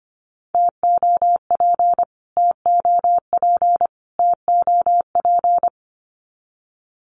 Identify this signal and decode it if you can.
Morse code, heard as a keyed tone, about 24 wpm, 700 Hz. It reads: TOP TOP TOP
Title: Top 500 Words Repeated 25wpm